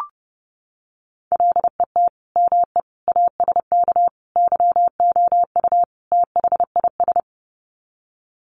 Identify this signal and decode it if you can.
Morse code, heard as a keyed tone, about 30 wpm, 700 Hz. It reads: LET ME ASK YOU THIS
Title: Sets of 5 Words 30wpm